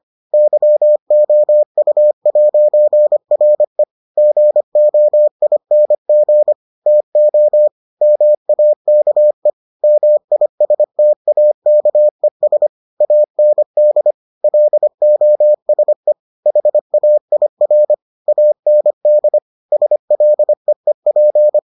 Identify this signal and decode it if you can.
Morse code, heard as a keyed tone, about 25 wpm, 600 Hz. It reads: YOU'RE GOING TO MAKE MISTAKES AND LOSE HAIR AND SLEEP